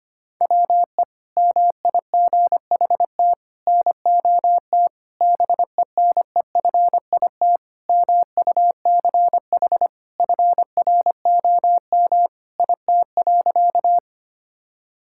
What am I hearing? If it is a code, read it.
Morse code, heard as a keyed tone, about 25 wpm, 700 Hz. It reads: WE MIGHT NOT BENEFIT MUCH FROM IT.
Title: Sentences from Top 1000 Words - Review Part 6 - 25wpm